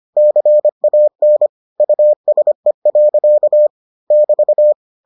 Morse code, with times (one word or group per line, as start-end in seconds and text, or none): 0.2-1.5 CAN
1.8-3.7 USE.
4.1-4.7 <BT>